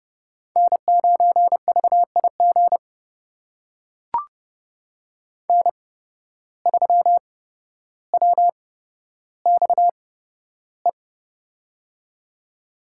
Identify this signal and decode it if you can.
Morse code, heard as a keyed tone, about 30 wpm, spaced as for 13 wpm, 700 Hz. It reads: N9VIG N3WXE
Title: USA Call Signs - 5 Characters - 8x Character Spacing - 30wpm